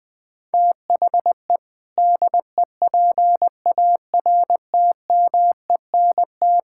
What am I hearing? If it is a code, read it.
Morse code, heard as a keyed tone, about 21 wpm, 700 Hz. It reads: THE DEPARTMENT
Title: Sentences from Top 1000 Words - Review Part 5 - 20wpm